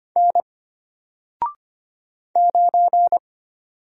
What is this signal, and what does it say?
Morse code, heard as a keyed tone, about 25 wpm, 700 Hz. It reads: N 9
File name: Single-Letter-Number-25wpm